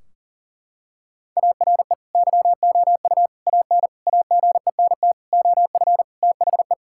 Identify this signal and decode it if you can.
Morse code, heard as a keyed tone, about 40 wpm, 700 Hz. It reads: ARE YOU AN AGENT OF THE
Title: Sentences from Top 1600 Words 40wpm